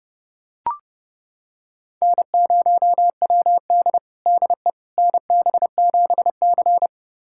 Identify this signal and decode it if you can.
Morse code, heard as a keyed tone, about 30 wpm, 700 Hz. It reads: N0WD DE NB7C
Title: USA Callsign DE Callsign - 4 Characters - 30wpm